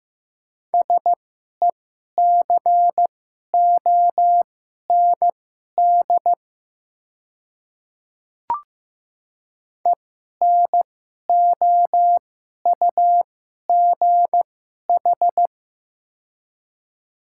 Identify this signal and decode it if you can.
Morse code, heard as a keyed tone, about 15 wpm, spaced as for 11 wpm, 700 Hz. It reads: SECOND ENOUGH
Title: 2x Character Spacing - Top 500 Words - 15wpm